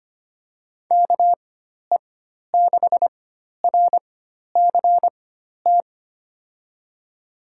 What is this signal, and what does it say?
Morse code, heard as a keyed tone, about 25 wpm, spaced as for 12 wpm, 700 Hz. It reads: KE6RCT
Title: USA Call Signs - 6 Characters - 4x Character Spacing - 25wpm